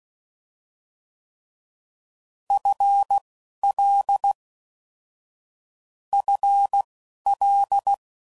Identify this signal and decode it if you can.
Morse code, heard as a keyed tone, about 16 wpm, spaced as for 12 wpm, 800 Hz. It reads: FL FL
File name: Basic Sets / QST de WA7RSO / Combnations